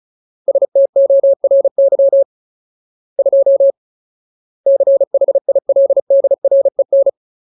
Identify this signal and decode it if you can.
Morse code, heard as a keyed tone, about 34 wpm, 550 Hz. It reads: STORY 2 CHILDREN
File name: Ultimate Word List / 35WPM Ultimate